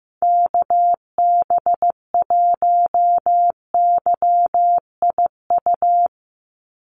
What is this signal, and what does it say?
Morse code, heard as a keyed tone, about 15 wpm, 700 Hz. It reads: KB1YIU
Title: USA Call Signs 15wpm